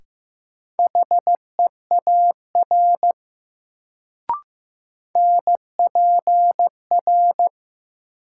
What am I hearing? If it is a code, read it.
Morse code, heard as a keyed tone, about 16 wpm, 700 Hz. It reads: HEAR NPR
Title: Sets of 5 Words - New Word Drill - 15wpm